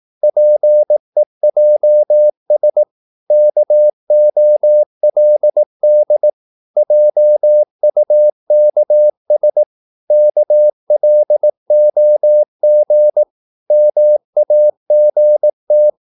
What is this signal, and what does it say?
Morse code, heard as a keyed tone, about 18 wpm, 600 Hz. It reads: PEJS KOLD JUKS KLOG MAGT
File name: Korte ord DK 18wpm